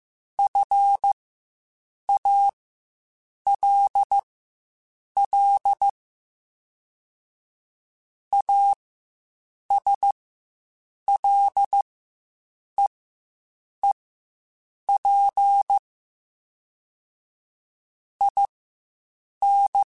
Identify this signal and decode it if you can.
Morse code, heard as a keyed tone, about 15 wpm, spaced as for 7 wpm, 800 Hz. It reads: FALL ASLEEP IN
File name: Quotes Spaced / Quotes Spaced-mp3s / ChildAsleep sp @10